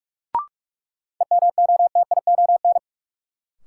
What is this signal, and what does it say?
Morse code, heard as a keyed tone, about 44 wpm, 700 Hz. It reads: EMOTION
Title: Sentences from Top 1600 Words - New Word Drill - 45wpm